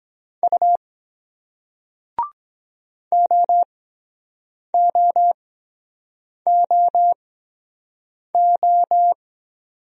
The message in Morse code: U OOOO

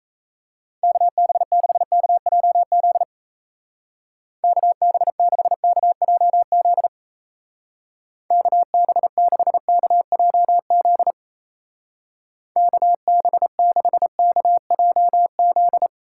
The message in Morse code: KB6KJZ KB6KJZ KB6KJZ KB6KJZ

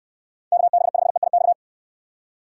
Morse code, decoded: KB5EIX

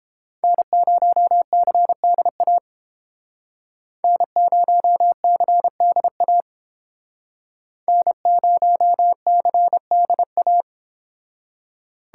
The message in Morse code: N0CDA N0CDA N0CDA

